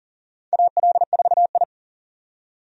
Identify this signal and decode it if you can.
Morse code, heard as a keyed tone, about 43 wpm, 690 Hz. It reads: AL4I
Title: USA Call Signs - 4 Characters - 40wpm